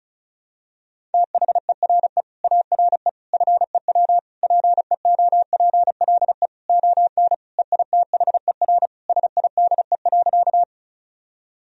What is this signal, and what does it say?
Morse code, heard as a keyed tone, about 36 wpm, 700 Hz. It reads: THERE ARE FEW PEOPLE ON EITHER SIDE.
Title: Sentences from Top 1000 Words - Review Part 4 - 35wpm